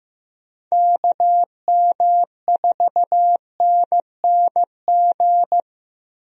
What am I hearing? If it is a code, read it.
Morse code, heard as a keyed tone, about 15 wpm, 700 Hz. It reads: KM4NNG
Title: USA Call Signs - 6 Characters - Repeated 2x - 15wpm